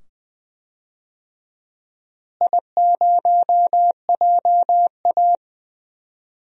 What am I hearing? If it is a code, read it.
Morse code, heard as a keyed tone, about 20 wpm, 700 Hz. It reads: I0JA
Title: DXCC-100 Call Signs - 4 Characters - Repeated 2x - 20wpm